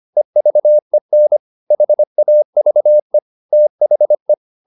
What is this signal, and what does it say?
Morse code, heard as a keyed tone, about 25 wpm, 600 Hz. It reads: EVEN HAVE THE